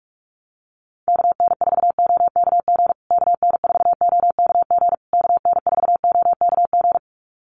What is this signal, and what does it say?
Morse code, heard as a keyed tone, about 44 wpm, 700 Hz. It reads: KN4OKG KN4OKG KN4OKG
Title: USA Call Signs Repeated 3x 45wpm